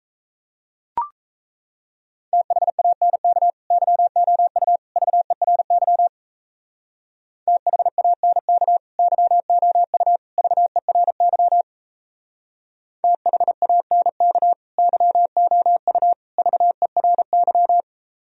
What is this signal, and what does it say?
Morse code, heard as a keyed tone, about 36 wpm, 700 Hz. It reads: THANK YOU VERY THANK YOU VERY THANK YOU VERY